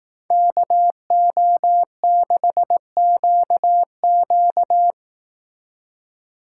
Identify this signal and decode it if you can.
Morse code, heard as a keyed tone, about 19 wpm, 700 Hz. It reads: KO6QQ